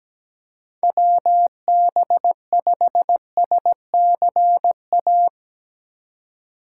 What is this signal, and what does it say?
Morse code, heard as a keyed tone, about 17 wpm, 700 Hz. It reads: WB5SCA
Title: USA Call Signs - 4x Character Spacing - 17wpm